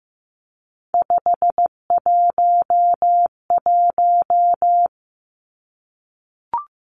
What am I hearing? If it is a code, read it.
Morse code, heard as a keyed tone, about 15 wpm, 700 Hz. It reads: 511
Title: Numbers - 3-Digit 15wpm